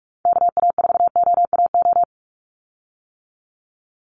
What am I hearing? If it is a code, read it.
Morse code, heard as a keyed tone, about 44 wpm, 700 Hz. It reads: KA4OAO
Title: USA Call Signs 45wpm